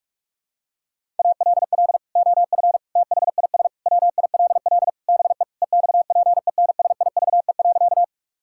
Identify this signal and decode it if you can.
Morse code, heard as a keyed tone, about 46 wpm, 700 Hz. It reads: ALL OF THIS WILL BE EXPENSIVE.
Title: Sentences from Top 1000 Words - Review Part 6 - 45wpm